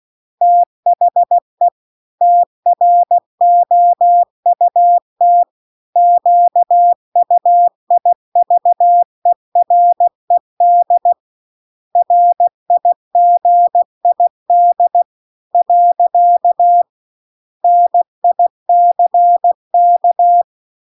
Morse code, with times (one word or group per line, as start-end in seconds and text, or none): 0.4-1.7 THE
2.2-5.4 TROUT
6.0-11.1 QUIVERED
12.0-15.0 RIGID
15.5-16.8 .
17.6-20.4 NICK